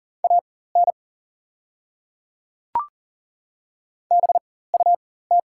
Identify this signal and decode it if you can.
Morse code, heard as a keyed tone, about 40 wpm, spaced as for 19 wpm, 690 Hz. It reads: AN BUT
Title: 4x Character Spacing - Top 100 Words - 40wpm